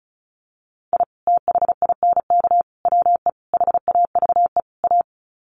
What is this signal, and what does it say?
Morse code, heard as a keyed tone, about 34 wpm, 700 Hz. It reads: I THINK WE HAVE A
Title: Sets of 5 Words Encore 35wpm